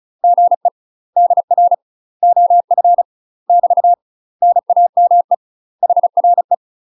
Morse code, heard as a keyed tone, about 35 wpm, 700 Hz. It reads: GE DR OF <BT> NAME HRE
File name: UK MASTER QSOs TRAINING / 35WPM / UK MASTER QSOs 0016